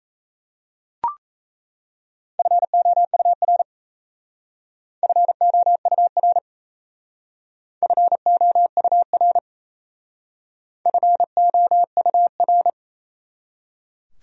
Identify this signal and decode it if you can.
Morse code, heard as a keyed tone, about 35 wpm, 700 Hz. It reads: FOUR FOUR FOUR FOUR
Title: Speed-Racing Top 500 Words 28wpm